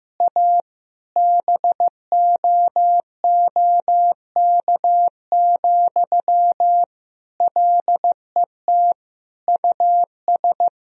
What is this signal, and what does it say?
Morse code, heard as a keyed tone, about 15 wpm, 700 Hz. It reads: A BOOK, LET US
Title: Sentences from Top 1000 Words - Review Part 4 - 15wpm